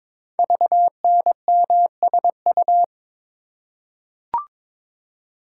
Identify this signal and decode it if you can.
Morse code, heard as a keyed tone, about 22 wpm, 700 Hz. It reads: VNMSU
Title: Code Groups - 5 Letters - 22wpm